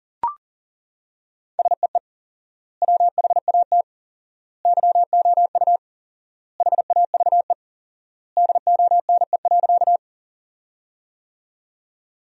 Morse code, heard as a keyed tone, about 37 wpm, 700 Hz. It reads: SEE WHAT YOU HAVE DONE.
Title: 4x Word-Spacing - Sentences from Top 500 Words - 40wpm